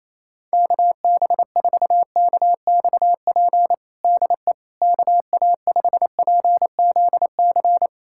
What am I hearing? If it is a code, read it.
Morse code, heard as a keyed tone, about 27 wpm, 700 Hz. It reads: KB4KXP DE KA5PZC
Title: USA Callsign DE Callsign - 6 Characters - 28wpm